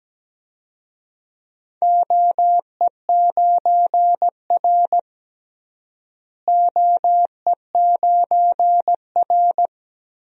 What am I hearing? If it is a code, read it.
Morse code, heard as a keyed tone, about 17 wpm, 700 Hz. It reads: OE9R OE9R
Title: DXCC-100 Call Signs - 4 Characters - Repeated 3x - 17wpm